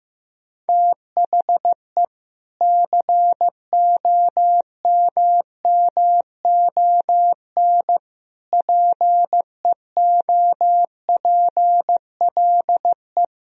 Morse code, heard as a keyed tone, about 15 wpm, 700 Hz. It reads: THE COMMON PEOPLE
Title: Sentences from Top 1000 Words - Review Part 3 - 15wpm